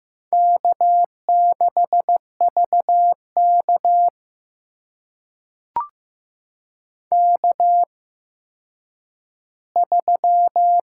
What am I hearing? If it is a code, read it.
Morse code, heard as a keyed tone, about 15 wpm, spaced as for 9 wpm, 700 Hz. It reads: K6VK K3
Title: USA Call Signs - 4 Characters - 8x Character Spacing - 15wpm